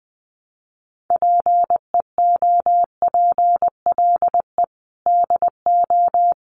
Morse code, heard as a keyed tone, about 19 wpm, 700 Hz. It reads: PEOPLE DO
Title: Sets of 5 Words Encore2 20wpm